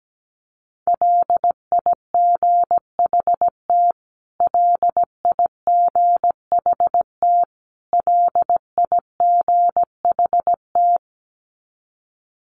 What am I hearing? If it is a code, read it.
Morse code, heard as a keyed tone, about 17 wpm, 700 Hz. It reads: LIGHT LIGHT LIGHT
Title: Top 200 Words Repeated 17wpm